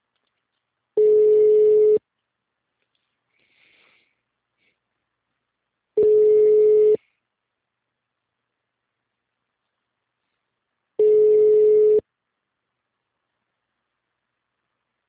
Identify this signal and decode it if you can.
Ringback tone (Europe, 425 Hz).